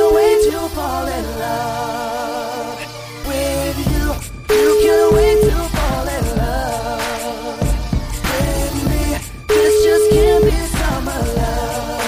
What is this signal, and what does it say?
Ringback tone (Europe, 425 Hz), with noise.